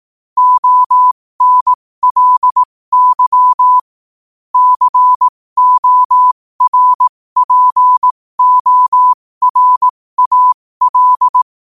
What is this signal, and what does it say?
Morse code, heard as a keyed tone, about 18 wpm, spaced as for 15 wpm, 1000 Hz. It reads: ONLY CORPORAL
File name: qotdfiles / 15wpm